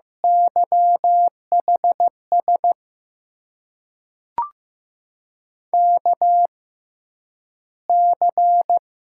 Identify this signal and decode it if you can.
Morse code, heard as a keyed tone, about 15 wpm, spaced as for 6 wpm, 700 Hz. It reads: YHSKC
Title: USA Call Signs - 6x Character Spacing - 15wpm